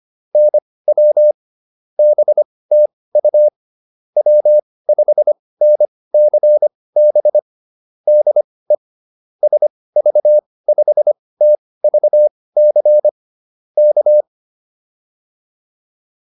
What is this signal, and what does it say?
Morse code, heard as a keyed tone, about 25 wpm, spaced as for 18 wpm, 600 Hz. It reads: NW BTU W5NCB DE SV5TVC K
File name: QSO408 18